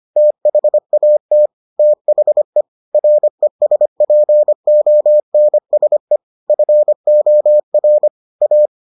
Morse code, heard as a keyed tone, about 25 wpm, 600 Hz. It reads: THAT THE RESPONSE FOR A